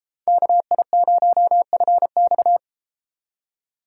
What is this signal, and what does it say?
Morse code, heard as a keyed tone, about 32 wpm, 700 Hz. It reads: KI0FX